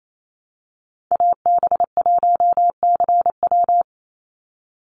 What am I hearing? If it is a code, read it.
Morse code, heard as a keyed tone, about 28 wpm, 700 Hz. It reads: AB1CW
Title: Call Signs Repeated 3x - 28wpm